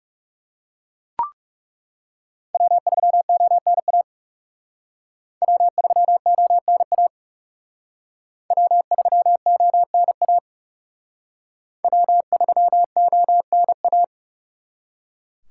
Morse code, heard as a keyed tone, about 37 wpm, 700 Hz. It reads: W3ONA W3ONA W3ONA W3ONA
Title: Speed Racing - USA 1x3 Call Signs 30wpm